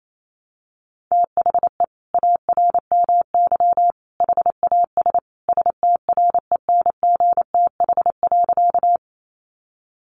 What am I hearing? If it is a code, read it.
Morse code, heard as a keyed tone, about 28 wpm, 700 Hz. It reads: THE ARMY HAS STRENGTH.
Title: Sentences from Top 1300 Words 28wpm